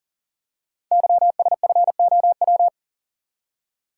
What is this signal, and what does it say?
Morse code, heard as a keyed tone, about 39 wpm, 700 Hz. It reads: YSFOW
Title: Code Groups - 5 Letter:Numbers - 40wpm